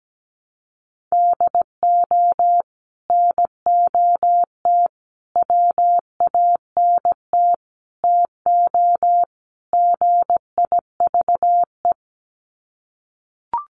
Morse code, heard as a keyed tone, about 17 wpm, 700 Hz. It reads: DO NOT WANT TO GIVE
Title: Sets of 5 Words Encore 17wpm